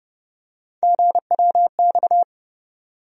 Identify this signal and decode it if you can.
Morse code, heard as a keyed tone, about 31 wpm, 700 Hz. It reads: GWX
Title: USA Call Sign Postfixes - 3 Letters 30wpm